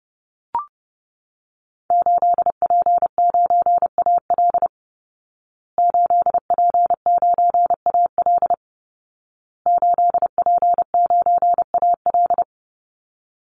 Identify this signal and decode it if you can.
Morse code, heard as a keyed tone, about 30 wpm, 700 Hz. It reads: 8P9AL 8P9AL 8P9AL
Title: Call Signs Repeated 2x - 30wpm